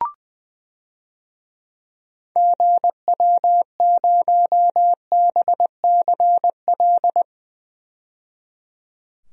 Morse code, heard as a keyed tone, about 21 wpm, 700 Hz. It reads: GW0BCL
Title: DXCC-100 Call Signs - 6 Characters - 20wpm